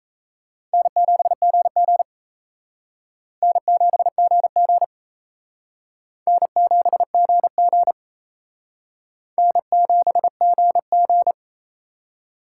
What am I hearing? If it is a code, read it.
Morse code, heard as a keyed tone, about 35 wpm, 700 Hz. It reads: N7GG N7GG N7GG N7GG
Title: Speed Racing - USA 1x2 Call Signs 28wpm